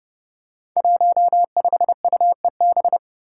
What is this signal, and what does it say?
Morse code, heard as a keyed tone, about 30 wpm, 700 Hz. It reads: 15UEB
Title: Five Challenging Letter-Number Combinations 30wpm